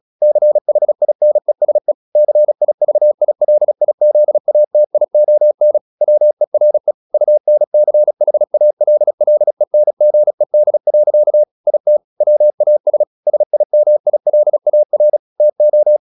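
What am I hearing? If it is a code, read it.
Morse code, heard as a keyed tone, about 36 wpm, 600 Hz. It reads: CHINESE CIVILIZATION WERE UNCHALLENGED. IT WAS SIMILAR TO
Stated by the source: Morse practice at 36 wpm, text of Ken Follett's novel Never, from CW med Gnister.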